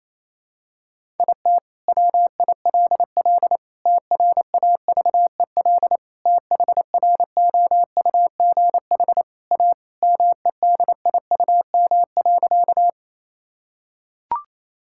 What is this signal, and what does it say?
Morse code, heard as a keyed tone, about 27 wpm, 700 Hz. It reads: IT WILL TRAVEL THROUGH A MEDIUM.
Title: Sentences from Top 1000 Words - Review Part 6 - 28wpm